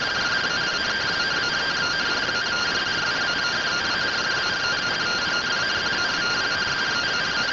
Two-tone teleprinter signal, about 170 Hz shift.
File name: findings / MR AXIAL 33,99mm 1.gen-25 rtty rx